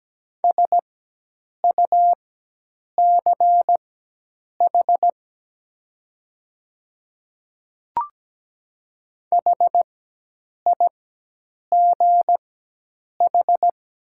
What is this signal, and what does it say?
Morse code, heard as a keyed tone, about 17 wpm, spaced as for 8 wpm, 700 Hz. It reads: SUCH HIGH
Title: 4x Character Spacing - Top 200 Words - 17wpm